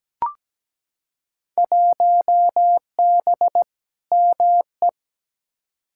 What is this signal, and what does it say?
Morse code, heard as a keyed tone, about 17 wpm, 700 Hz. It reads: 1B ME
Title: Field Day Exchanges - 17wpm